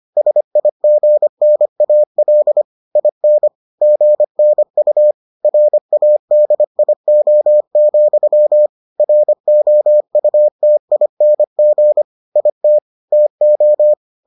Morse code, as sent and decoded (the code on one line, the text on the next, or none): ... .. --. -. .- .-.. / .. -. / --. -. ..- / .-. .- -.. .. --- --..-- / .-. --- ..- - .. -. --. / .. - / - ---
SIGNAL IN GNU RADIO, ROUTING IT TO